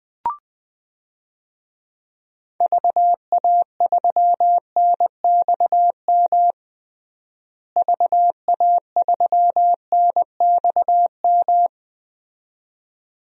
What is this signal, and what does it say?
Morse code, heard as a keyed tone, about 21 wpm, 700 Hz. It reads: VA3NXM VA3NXM
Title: DXCC-100 Call Signs - 6 Characters - Repeated 2x - 20wpm